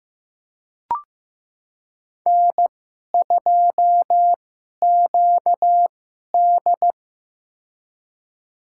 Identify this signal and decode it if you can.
Morse code, heard as a keyed tone, about 15 wpm, spaced as for 11 wpm, 700 Hz. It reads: N2QD
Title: USA Call Signs - 4 Characters - 2x Character Spacing - 15wpm